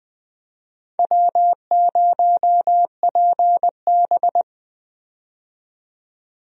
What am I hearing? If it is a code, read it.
Morse code, heard as a keyed tone, about 20 wpm, 700 Hz. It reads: W0PB
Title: USA Call Signs - 4 Characters - 20wpm